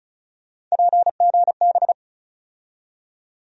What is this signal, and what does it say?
Morse code, heard as a keyed tone, about 34 wpm, 700 Hz. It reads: PGB
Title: USA Call Sign Postfixes - 3 Letters 35wpm